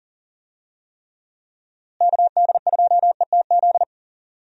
Morse code, heard as a keyed tone, about 38 wpm, 700 Hz. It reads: KD2ETZ
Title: USA Call Signs - 6 Characters - Repeated 3x - 40wpm